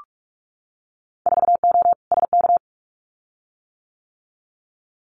Morse code, heard as a keyed tone, about 46 wpm, 700 Hz. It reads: 4O SK
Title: Winter Field Day Exchanges Encore 45wpm